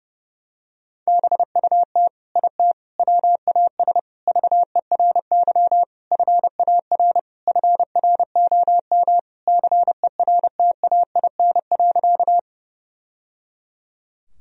Morse code, heard as a keyed tone, about 30 wpm, 700 Hz. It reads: BUT IT WAS VERY FAR FROM CERTAIN.
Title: Sentences from Top 1000 Words - Review Part 2 - 30wpm